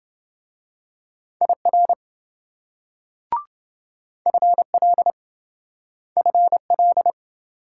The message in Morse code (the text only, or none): IR FL FL